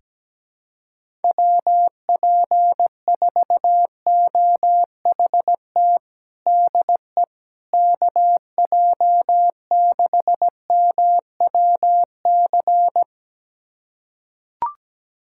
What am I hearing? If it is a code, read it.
Morse code, heard as a keyed tone, about 17 wpm, 700 Hz. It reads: WP4OHT DE KJ6MWC
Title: USA Callsign DE Callsign 17wpm